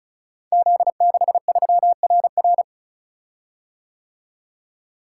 Morse code, heard as a keyed tone, about 36 wpm, 700 Hz. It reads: Z63RR
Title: Code Groups - 5 Letter:Numbers - 35wpm